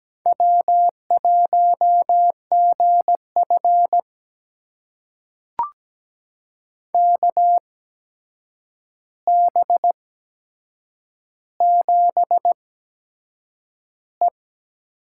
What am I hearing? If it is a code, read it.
Morse code, heard as a keyed tone, about 17 wpm, spaced as for 7 wpm, 700 Hz. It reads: W1GF KB7E
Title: USA Call Signs - 8x Character Spacing - 17wpm